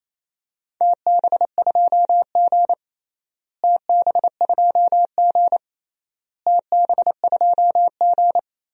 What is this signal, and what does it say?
Morse code, heard as a keyed tone, about 28 wpm, 700 Hz. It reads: TB2G TB2G TB2G
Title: DXCC-100 Call Signs - 4 Characters - Repeated 3x - 28wpm